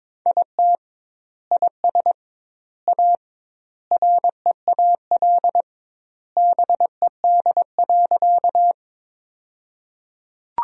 Morse code, heard as a keyed tone, about 22 wpm, 700 Hz. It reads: IT IS A REAL BED.
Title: 2x Word-Spacing - Sentences from Top 500 Words - 22wpm